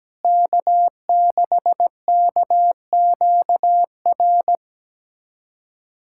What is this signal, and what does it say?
Morse code, heard as a keyed tone, about 17 wpm, 700 Hz. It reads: K6KQR
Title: USA Callsign DE Callsign - 5 Characters - 17wpm